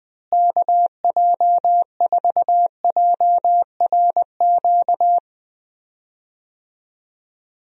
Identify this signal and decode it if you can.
Morse code, heard as a keyed tone, about 20 wpm, 700 Hz. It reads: KJ4JRQ
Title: USA Call Signs - 6 Characters - Repeated 2x - 20wpm